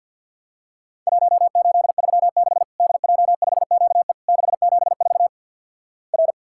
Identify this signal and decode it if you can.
Morse code, heard as a keyed tone, about 52 wpm, 700 Hz. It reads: 1836 DJ5QE 674 R
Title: Worked All Europe DX Contest - QTC Exchange with Repeats - 50wpm